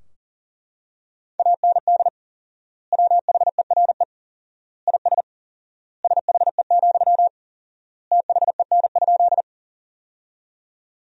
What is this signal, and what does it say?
Morse code, heard as a keyed tone, about 40 wpm, 700 Hz. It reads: AND WHERE IS SHE, THEN?
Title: 4x Word-Spacing - Sentences from Top 200 Words - 40wpm